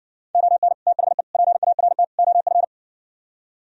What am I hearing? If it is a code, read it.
Morse code, heard as a keyed tone, about 60 wpm, 700 Hz. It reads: ON THE PART OF